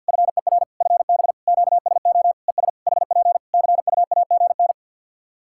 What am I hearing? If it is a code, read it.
Morse code, heard as a keyed tone, about 50 wpm, 700 Hz. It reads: FER FB QSO ES HP CUAGN
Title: K1USN SST QSO Practice 50wpm